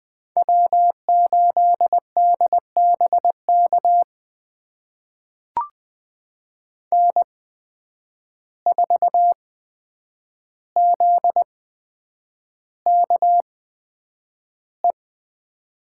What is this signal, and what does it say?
Morse code, heard as a keyed tone, about 20 wpm, spaced as for 9 wpm, 700 Hz. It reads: W8DBK N4ZKE
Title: USA Call Signs - 5 Characters - 8x Character Spacing - 20wpm